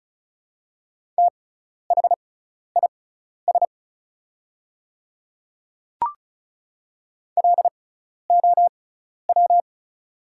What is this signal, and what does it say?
Morse code, heard as a keyed tone, about 34 wpm, spaced as for 12 wpm, 700 Hz. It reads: THIS LOW
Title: 6x Character Spacing - Top 300 Words - 35wpm